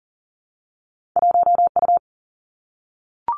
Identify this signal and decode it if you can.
Morse code, heard as a keyed tone, about 42 wpm, 700 Hz. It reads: 1U
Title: Two Challenging Letter-Number Combinations 40wpm